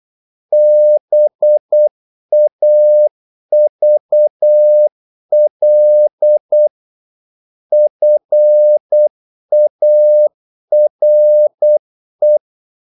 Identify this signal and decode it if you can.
Morse code, heard as a keyed tone, about 8 wpm, 600 Hz. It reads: BAVL FARE
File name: Korte ord DK 8wpm